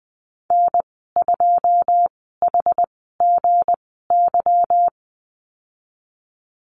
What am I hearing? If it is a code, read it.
Morse code, heard as a keyed tone, about 20 wpm, spaced as for 14 wpm, 700 Hz. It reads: N2HGY
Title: 167 Lesson 27g - Number 7 - Call Signs 20wpm